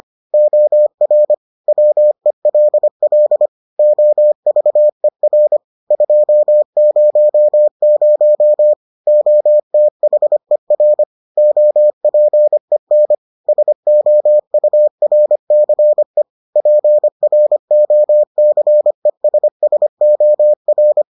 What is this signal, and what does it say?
Morse code, heard as a keyed tone, about 25 wpm, 600 Hz. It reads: OR WELL OVER 200 OTHER OPEN SOURCE PROCESSOR